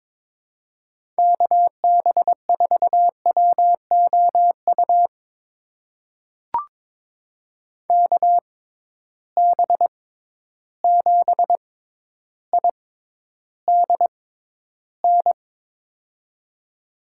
Morse code, heard as a keyed tone, about 23 wpm, spaced as for 11 wpm, 700 Hz. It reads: KB4WOU KB7IDN
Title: USA Call Signs - 6 Characters - 6x Character Spacing - 22wpm